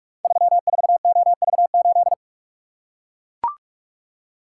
Morse code, heard as a keyed tone, about 44 wpm, 700 Hz. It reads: 34OV8
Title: Code Groups - 5 Letter:Numbers - 45wpm